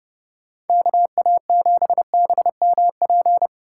Morse code, heard as a keyed tone, about 30 wpm, 700 Hz. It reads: KA7BMP
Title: USA Call Signs - 6 Characters - 4x Character Spacing - 30wpm